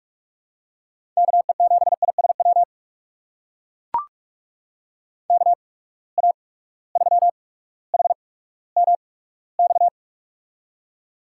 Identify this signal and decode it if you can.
Morse code, heard as a keyed tone, about 44 wpm, spaced as for 19 wpm, 700 Hz. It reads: KE7ISW KA3HMX
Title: USA Call Signs - 6 Characters - 8x Character Spacing - 45wpm